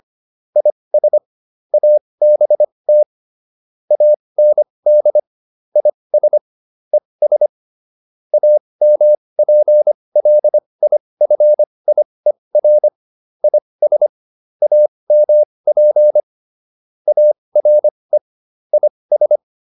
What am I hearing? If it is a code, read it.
Morse code, heard as a keyed tone, about 25 wpm, spaced as for 20 wpm, 600 Hz. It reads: IS ABT AND IS ES AMPLIFIER IS AMP ARE IS